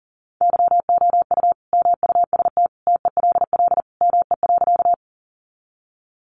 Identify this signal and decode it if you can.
Morse code, heard as a keyed tone, about 40 wpm, 700 Hz. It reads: YOU MUST TELL ME.
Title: Sentences from Top 200 Words 40wpm